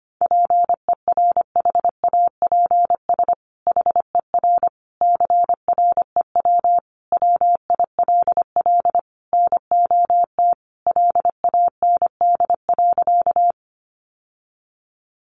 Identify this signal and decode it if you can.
Morse code, heard as a keyed tone, about 25 wpm, 700 Hz. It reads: PERHAPS HER CREW WILL NOT LAND.
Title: Sentences from Top 1500 Words 25wpm